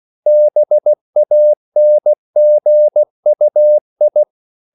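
Morse code, heard as a keyed tone, about 16 wpm, 600 Hz. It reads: BANGUI
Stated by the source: CW practice of national capitals at 16 words per minute, from ON6ZQ.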